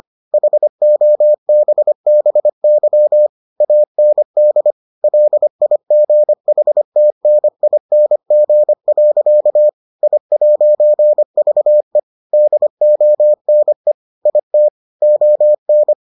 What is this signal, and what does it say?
Morse code, heard as a keyed tone, about 25 wpm, 600 Hz. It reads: HOBBY AND LIGHTNING. I'VE DONE IT ON